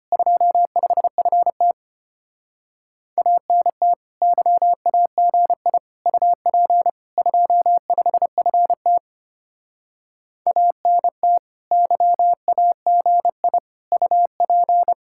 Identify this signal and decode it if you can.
Morse code, heard as a keyed tone, about 33 wpm, 700 Hz. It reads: 25FT ANT YAGI UP 25FT ANT YAGI UP